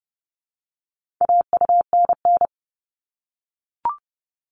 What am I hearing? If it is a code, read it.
Morse code, heard as a keyed tone, about 30 wpm, 700 Hz. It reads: AUNN
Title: All Cut Numbers - 4-Digit 30wpm